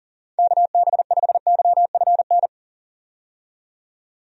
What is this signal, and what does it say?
Morse code, heard as a keyed tone, about 37 wpm, 700 Hz. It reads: KB5YFN
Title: USA Call Signs - 6 Characters - 40wpm